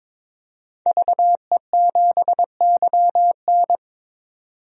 Morse code, heard as a keyed tone, about 22 wpm, 700 Hz. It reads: VE7YN
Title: DXCC-100 Call Signs - 5 Characters - 22wpm